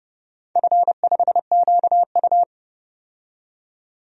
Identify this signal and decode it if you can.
Morse code, heard as a keyed tone, about 32 wpm, 700 Hz. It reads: F5QU
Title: DXCC-100 Call Signs - 4 Characters - 30wpm